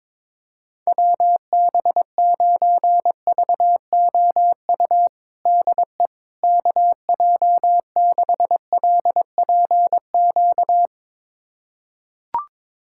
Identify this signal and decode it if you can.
Morse code, heard as a keyed tone, about 22 wpm, 700 Hz. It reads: WB9VOU DE KJ6LPQ
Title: USA Callsign DE Callsign - 6 Characters - 22wpm